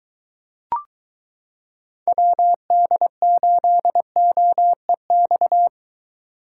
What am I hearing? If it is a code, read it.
Morse code, heard as a keyed tone, about 22 wpm, 700 Hz. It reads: WD8OEX